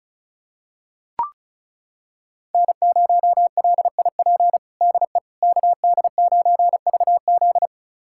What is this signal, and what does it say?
Morse code, heard as a keyed tone, about 36 wpm, 700 Hz. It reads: N0LIP DE KD9VZ
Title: USA Callsign DE Callsign - 5 Characters - 35wpm